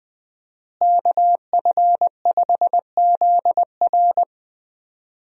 Morse code, heard as a keyed tone, about 21 wpm, 700 Hz. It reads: KF5ZR
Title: USA Call Signs - 5 Characters - Repeated 2x - 20wpm